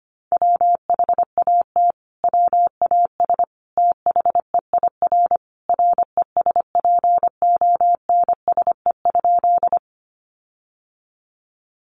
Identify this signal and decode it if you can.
Morse code, heard as a keyed tone, about 25 wpm, 700 Hz. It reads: WHAT WAS THEIR RESPONSE?
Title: Sentences from Top 1400 Words 25wpm